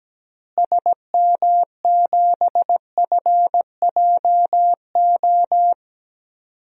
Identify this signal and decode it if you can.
Morse code, heard as a keyed tone, about 17 wpm, 700 Hz. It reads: SM7FJO
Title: DXCC-100 Call Signs - 6 Characters - Repeated 2x - 17wpm